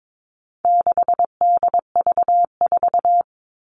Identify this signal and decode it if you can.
Morse code, heard as a keyed tone, about 22 wpm, 700 Hz. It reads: 6DV4
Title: All Cut Numbers - 4-Digit 22wpm